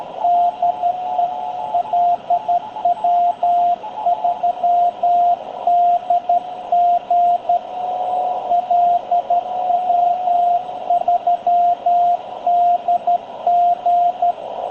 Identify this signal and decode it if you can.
Morse code, heard as a keyed tone, about 12 wpm, 700 Hz, with radio noise. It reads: DE LW3DG LW3DG
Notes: Encuentro de radioaficionados en 7030, banda de 40 m, batiendo manipuladores, para despedir a Tony, CA2LQA, de Villa Alemana, Valparaíso, Chile.